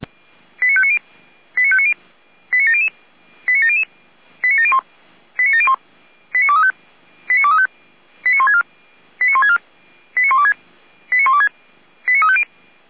Five-tone selective call (ZVEI), heard as quick runs of five tones, tones of about 70 ms. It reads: 89590 89590 89700 89700 89701 89701 89235 89235 89164 89164 89126 89126 89350
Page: /455/SDR/ZVEI/